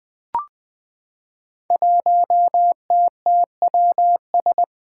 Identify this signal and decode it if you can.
Morse code, heard as a keyed tone, about 20 wpm, 700 Hz. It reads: 1TTWS
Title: Five Challenging Letter-Number Combinations 20wpm